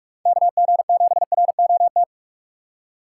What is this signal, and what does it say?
Morse code, heard as a keyed tone, about 46 wpm, 700 Hz. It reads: KG7ROT